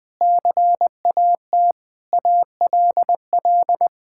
Morse code, heard as a keyed tone, about 21 wpm, 700 Hz. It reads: CAT ALL
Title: Sentences with 4-Word Phrases - 20wpm